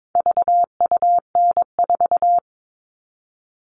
Morse code, heard as a keyed tone, about 22 wpm, 700 Hz. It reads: VUN4
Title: All Cut Numbers - 4-Digit 22wpm